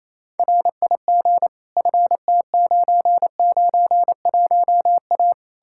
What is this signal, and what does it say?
Morse code, heard as a keyed tone, about 37 wpm, 700 Hz. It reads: RIG FT991A